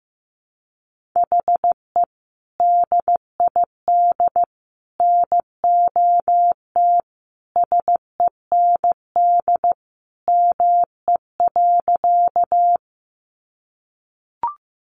Morse code, heard as a keyed tone, about 15 wpm, 700 Hz. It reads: HE DID NOT SEND ME.
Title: Sentences from Top 1500 Words 15wpm